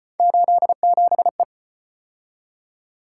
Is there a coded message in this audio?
Yes: Morse code from 0.2 to 1.4 s, heard as a keyed tone, about 33 wpm, 700 Hz. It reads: 87E